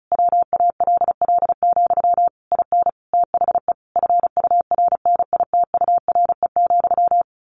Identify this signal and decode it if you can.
Morse code, heard as a keyed tone, about 34 wpm, 700 Hz. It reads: WALL, IN THE FURNITURE,
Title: A Sound Of Thunder 35wpm